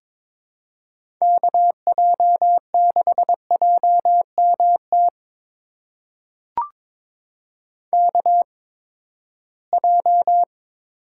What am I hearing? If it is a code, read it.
Morse code, heard as a keyed tone, about 21 wpm, spaced as for 15 wpm, 700 Hz. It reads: KJ6JMT KJ
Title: USA Call Signs - 6 Characters - 8x Character Spacing - 22wpm